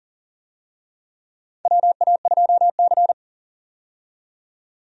Morse code, heard as a keyed tone, about 40 wpm, 700 Hz. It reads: WA2C
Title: USA Call Signs - 4 Characters - 40wpm